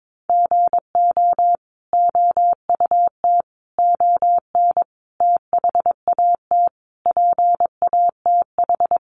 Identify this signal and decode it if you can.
Morse code, heard as a keyed tone, about 22 wpm, 700 Hz. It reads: GO OUT ON THAT PATH
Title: A Sound Of Thunder 22wpm